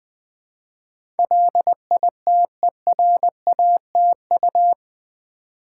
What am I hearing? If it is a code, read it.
Morse code, heard as a keyed tone, about 20 wpm, 700 Hz. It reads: LITERATU
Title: Build A Word - 4 Letters At A Time - 10-Letter Words - 20wpm